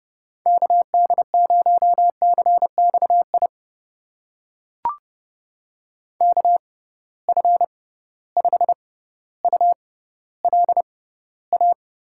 Morse code, heard as a keyed tone, about 31 wpm, spaced as for 16 wpm, 700 Hz. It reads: KD0CXI KF5ULA
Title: USA Call Signs - 6 Characters - 6x Character Spacing - 30wpm